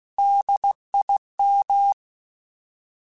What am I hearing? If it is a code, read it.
Morse code, heard as a keyed tone, about 16 wpm, 800 Hz. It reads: DIM
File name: Alphabetical / Alphabetical mp3s / Ds @15